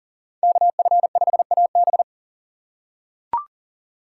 Morse code, heard as a keyed tone, about 37 wpm, 700 Hz. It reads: KF5AB
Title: USA Call Signs - 5 Characters - 4x Character Spacing - 40wpm